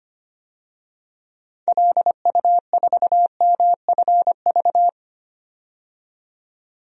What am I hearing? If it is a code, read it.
Morse code, heard as a keyed tone, about 25 wpm, 700 Hz. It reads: LU4MFV
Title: DXCC-100 Call Signs - 6 Characters - 25wpm